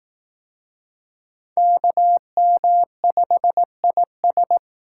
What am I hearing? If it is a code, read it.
Morse code, heard as a keyed tone, about 19 wpm, 700 Hz. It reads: KM5IS
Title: Speed Racing - USA 2x2 Call Signs 15wpm